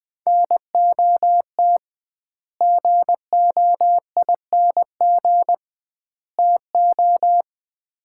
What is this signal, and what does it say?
Morse code, heard as a keyed tone, about 20 wpm, 700 Hz. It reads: NOT GOING TO
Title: 2x Word-Spacing - Sets of 3 Words - 20wpm